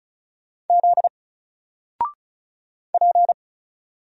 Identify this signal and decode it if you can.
Morse code, heard as a keyed tone, about 36 wpm, 700 Hz. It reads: Z P
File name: Single-Letter-Number-35wpm